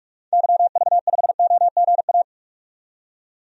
Morse code, heard as a keyed tone, about 44 wpm, 690 Hz. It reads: YV5OGA
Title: DXCC-100 Call Signs - 6 Characters - 45wpm